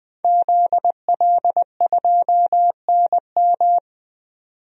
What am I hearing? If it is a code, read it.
Morse code, heard as a keyed tone, about 20 wpm, 700 Hz. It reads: ZL2NM
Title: DXCC-100 Call Signs - 5 Characters - 20wpm